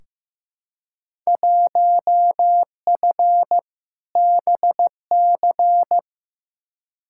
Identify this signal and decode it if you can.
Morse code, heard as a keyed tone, about 15 wpm, 700 Hz. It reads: 1F BC
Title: Field Day Exchanges Encore - 15wpm